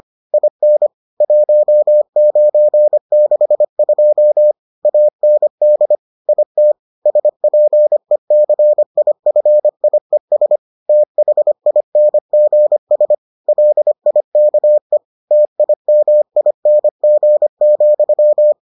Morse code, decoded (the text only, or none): IN 1962 AND IT SPECIFIES THINGS LIKE TIMING,